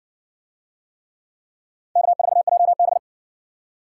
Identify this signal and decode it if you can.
Morse code, heard as a keyed tone, about 57 wpm, 700 Hz. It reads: K4JB